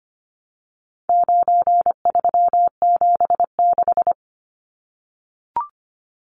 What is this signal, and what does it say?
Morse code, heard as a keyed tone, about 25 wpm, 700 Hz. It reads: 9376
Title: Numbers - 4-Digit 25wpm